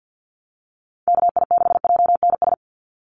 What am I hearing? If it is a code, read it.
Morse code, heard as a keyed tone, about 50 wpm, 700 Hz. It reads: KI6JNS
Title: USA Call Signs 50wpm